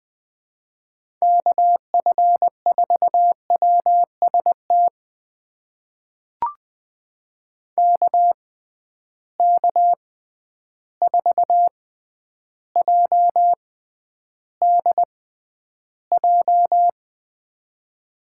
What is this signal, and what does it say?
Morse code, heard as a keyed tone, about 20 wpm, spaced as for 10 wpm, 700 Hz. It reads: KF4WST KK4JDJ
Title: USA Call Signs - 6 Characters - 6x Character Spacing - 20wpm